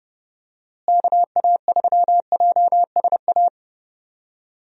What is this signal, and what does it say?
Morse code, heard as a keyed tone, about 32 wpm, 700 Hz. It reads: KA3JSA
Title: USA Call Signs - 6 Characters - Repeated 3x - 30wpm